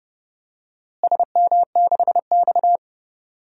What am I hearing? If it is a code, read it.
Morse code, heard as a keyed tone, about 30 wpm, 700 Hz. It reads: SM6X
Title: DXCC-100 Call Signs - 4 Characters - Repeated 2x - 30wpm